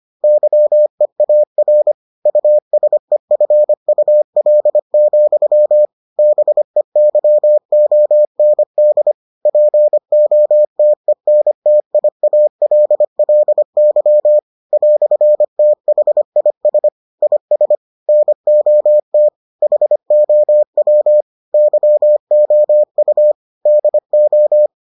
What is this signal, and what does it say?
Morse code, heard as a keyed tone, about 25 wpm, 600 Hz. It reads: YEAR USEFUL, BEYOND POTENTIALLY "THIS IS NOT HOW YOU DO